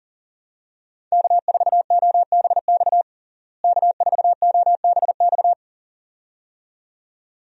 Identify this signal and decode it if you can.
Morse code, heard as a keyed tone, about 40 wpm, 700 Hz. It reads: K4OBX K4OBX
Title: USA Call Signs - 5 Characters - Repeated 2x - 40wpm